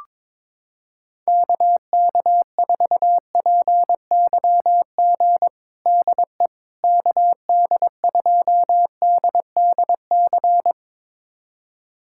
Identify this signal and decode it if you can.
Morse code, heard as a keyed tone, about 22 wpm, 700 Hz. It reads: KK4PYG DE KD2DDC
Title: USA Callsign DE Callsign 22wpm